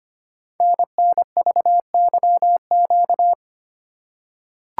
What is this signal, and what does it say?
Morse code, heard as a keyed tone, about 25 wpm, 700 Hz. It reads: NNVYQ
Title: Code Groups - 5 Letters - 25wpm